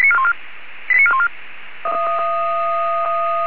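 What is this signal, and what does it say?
Five-tone selective call (ZVEI), heard as quick runs of five tones, tones of about 70 ms. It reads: 89414
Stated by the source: Received on SDR, ZVEI five-tone call, address 89411.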